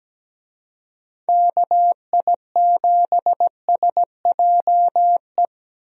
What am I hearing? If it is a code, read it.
Morse code, heard as a keyed tone, about 17 wpm, 700 Hz. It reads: KI7SJE
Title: USA Call Signs - 6 Characters - 17wpm